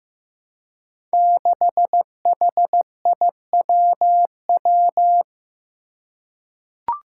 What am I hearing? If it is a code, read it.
Morse code, heard as a keyed tone, about 15 wpm, 700 Hz. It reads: 6HIWW
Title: Five Challenging Letter-Number Combinations 15wpm